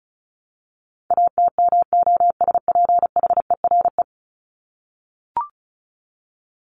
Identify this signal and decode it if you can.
Morse code, heard as a keyed tone, about 34 wpm, 700 Hz. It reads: ATMOSPHERE
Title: Top 1501-1750 Words 35wpm